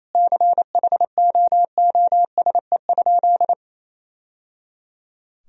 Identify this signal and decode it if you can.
Morse code, heard as a keyed tone, about 28 wpm, 700 Hz. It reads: CHOOSE?
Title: Sentences from Top 1000 Words - Review Part 4 - 28wpm